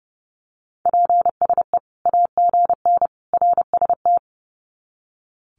Morse code, heard as a keyed tone, about 30 wpm, 700 Hz. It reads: PSE AGN RST
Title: Common QSO Phrases 30wpm